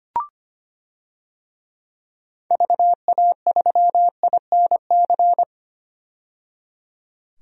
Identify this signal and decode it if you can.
Morse code, heard as a keyed tone, about 25 wpm, 700 Hz. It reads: VA3INC
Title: DXCC-100 Call Signs - 6 Characters - 25wpm